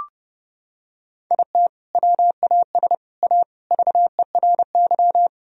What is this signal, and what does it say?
Morse code, heard as a keyed tone, about 32 wpm, 700 Hz. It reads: IT WAS A VERY
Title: Sentences from Top 1000 Words - Review Part 1 - 30wpm